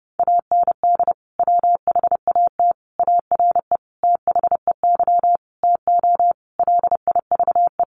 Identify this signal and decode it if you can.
Morse code, heard as a keyed tone, about 30 wpm, 700 Hz. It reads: AND WHAT ARE THEY TO LIVE
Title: Sentences from Top 1100 Words 30wpm